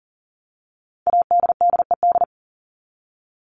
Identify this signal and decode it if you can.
Morse code, heard as a keyed tone, about 40 wpm, 700 Hz. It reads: ADDED
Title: ED Suffix - Top 100 Words with Base Words 40wpm